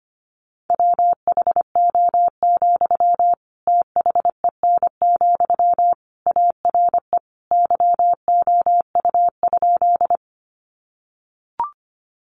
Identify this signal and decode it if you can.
Morse code, heard as a keyed tone, about 25 wpm, 700 Hz. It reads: WHO, THEN, ARE YOU?
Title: Sentences from Top 100 Words 25wpm